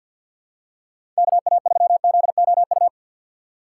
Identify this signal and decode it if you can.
Morse code, heard as a keyed tone, about 53 wpm, 700 Hz. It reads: KA3ZOU
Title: USA Call Signs - 6 Characters - 8x Character Spacing - 50wpm